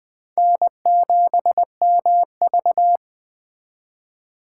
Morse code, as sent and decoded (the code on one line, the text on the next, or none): -. --... -- ...-
N7MV